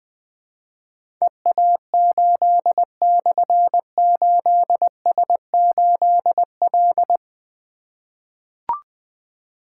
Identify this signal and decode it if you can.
Morse code, heard as a keyed tone, about 20 wpm, 700 Hz. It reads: EA8/8S8L
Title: Difficult Call Signs 20wpm